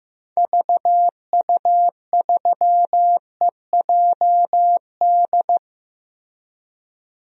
Morse code, heard as a keyed tone, about 15 wpm, 700 Hz. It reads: VU3EJD
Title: DXCC-100 Call Signs - 6 Characters - Repeated 3x - 15wpm